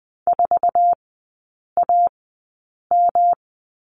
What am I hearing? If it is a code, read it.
Morse code, heard as a keyed tone, about 20 wpm, spaced as for 8 wpm, 700 Hz. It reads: 4AM
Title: 133 Lesson 22e - Question Mark - Three Characters 20wpm